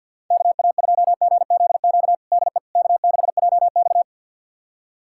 Morse code, heard as a keyed tone, about 50 wpm, 700 Hz. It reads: KA2GZQ DE K6JX
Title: USA Callsign DE Callsign 50wpm